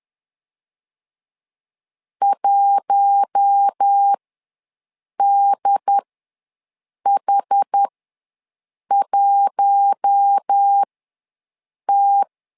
Morse code, heard as a keyed tone, about 10 wpm, spaced as for 6 wpm, 800 Hz. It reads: 1DH1T
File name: LEZIONI / 4 Q1 ws 10wpm 2538